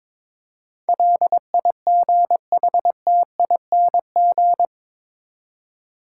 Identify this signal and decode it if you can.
Morse code, heard as a keyed tone, about 22 wpm, 700 Hz. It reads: LIGHTING
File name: practice / 22wpm / Eight Letter Words - 22wpm